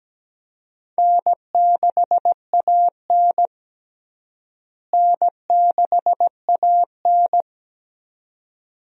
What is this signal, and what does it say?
Morse code, heard as a keyed tone, about 17 wpm, 700 Hz. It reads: N6AN N6AN
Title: USA Call Signs - 4 Characters - Repeated 2x - 17wpm